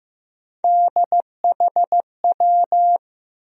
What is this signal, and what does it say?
Morse code, heard as a keyed tone, about 15 wpm, 700 Hz. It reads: DHW
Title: USA Call Sign Postfixes - 3 Letters 15wpm